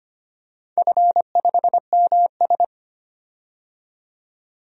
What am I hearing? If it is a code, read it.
Morse code, heard as a keyed tone, about 25 wpm, 700 Hz. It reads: F5MS
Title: DXCC-100 Call Signs - 4 Characters - 25wpm